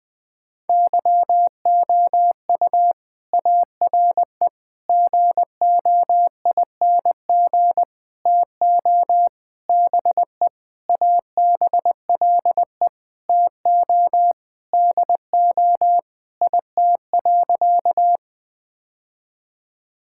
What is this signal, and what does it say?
Morse code, heard as a keyed tone, about 20 wpm, 700 Hz. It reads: YOU ARE GOING TO BE ABLE TO DO IT.
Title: Sentences with 4-Word Phrases - Spoken Beginning - 20wpm